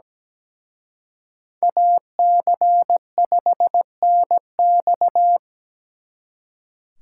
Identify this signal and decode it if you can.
Morse code, heard as a keyed tone, about 20 wpm, 700 Hz. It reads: AC5NX
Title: Speed Racing - USA 2x2 Call Signs 17wpm